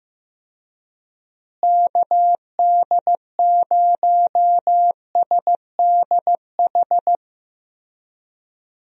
Morse code, heard as a keyed tone, about 15 wpm, 700 Hz. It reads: KD0SDH
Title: USA Call Signs - 6 Characters - Repeated 3x - 15wpm